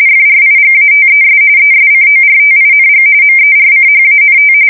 RTTY, about 45.45 baud. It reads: FRIGATES ESCORTING MERCHANT
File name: fsk170 / usnavy2